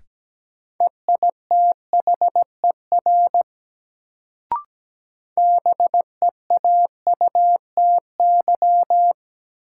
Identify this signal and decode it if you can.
Morse code, heard as a keyed tone, about 17 wpm, 700 Hz. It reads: EITHER BEAUTY
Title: Sentences from Top 600 Words - New Word Drill - 17wpm